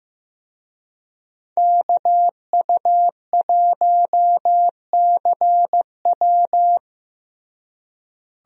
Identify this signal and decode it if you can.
Morse code, heard as a keyed tone, about 15 wpm, 700 Hz. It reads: KU1CW
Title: USA Call Signs - 5 Characters - Repeated 3x - 15wpm